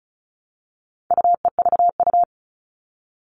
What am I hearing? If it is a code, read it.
Morse code, heard as a keyed tone, about 34 wpm, 700 Hz. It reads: UEVU
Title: All Cut Numbers - 4-Digit 35wpm